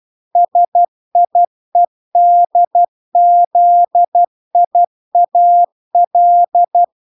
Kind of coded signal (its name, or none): Morse code